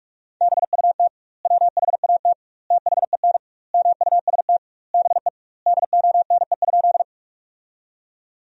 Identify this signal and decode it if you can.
Morse code, heard as a keyed tone, about 46 wpm, 700 Hz. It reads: BUT WHAT THEN MUST BE DONE?
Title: 2x Word-Spacing - Sentences from Top 500 Words - 45wpm